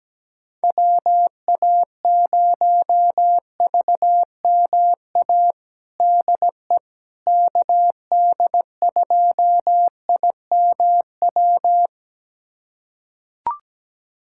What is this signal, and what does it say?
Morse code, heard as a keyed tone, about 17 wpm, 700 Hz. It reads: WA0VMA DE KD2IMW
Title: USA Callsign DE Callsign 17wpm